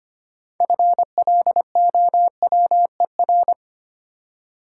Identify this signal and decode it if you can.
Morse code, heard as a keyed tone, about 25 wpm, 700 Hz. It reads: FLOWER
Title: Six Letter Words - 25wpm